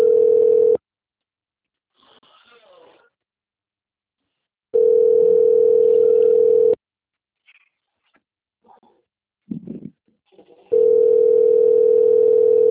Ringback tone (North America), 440 + 480 Hz.
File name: out- - - - - .9939